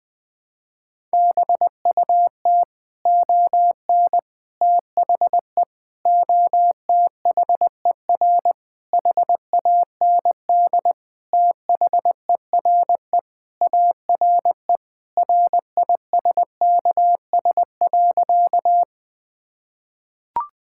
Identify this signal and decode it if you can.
Morse code, heard as a keyed tone, about 21 wpm, 700 Hz. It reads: BUT ON THE OTHER HAND THERE ARE RISKS.
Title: Sentences with 5-Word Phrases - 20wpm